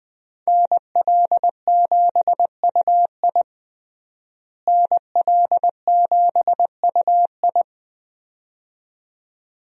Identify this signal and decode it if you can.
Morse code, heard as a keyed tone, about 20 wpm, 700 Hz. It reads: NL7UI NL7UI
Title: USA Call Signs - 5 Characters - Repeated 2x - 20wpm